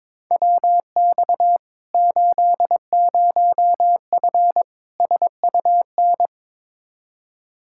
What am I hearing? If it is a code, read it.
Morse code, heard as a keyed tone, about 22 wpm, 700 Hz. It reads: WX 80F SUN
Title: Practice QSO - Exchange 2 - Info - 22wpm